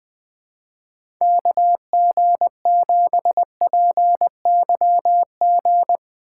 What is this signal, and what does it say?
Morse code, heard as a keyed tone, about 21 wpm, 700 Hz. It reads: KG7PYG
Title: USA Call Signs - 2x Character Spacing - 20wpm